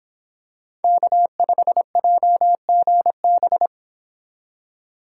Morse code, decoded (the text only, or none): K5JGB